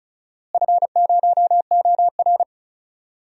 Morse code, heard as a keyed tone, about 34 wpm, 700 Hz. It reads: F0OR
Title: DXCC-100 Call Signs - 4 Characters - Repeated 2x - 35wpm